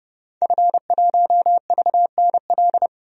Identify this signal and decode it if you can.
Morse code, heard as a keyed tone, about 30 wpm, 700 Hz. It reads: F1VNL
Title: DXCC-100 Call Signs - 5 Characters - 30wpm